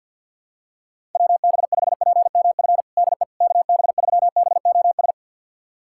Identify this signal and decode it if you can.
Morse code, heard as a keyed tone, about 52 wpm, 700 Hz. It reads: WB5PMF DE KB3BOS
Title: USA Callsign DE Callsign 50wpm